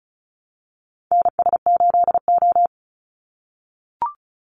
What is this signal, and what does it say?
Morse code, heard as a keyed tone, about 34 wpm, 700 Hz. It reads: NS8O
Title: Call Signs - 35wpm